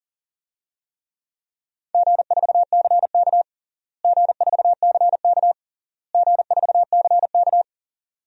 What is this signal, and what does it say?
Morse code, heard as a keyed tone, about 37 wpm, 700 Hz. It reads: G4CK G4CK G4CK
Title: DXCC-100 Call Signs - 4 Characters - Repeated 3x - 40wpm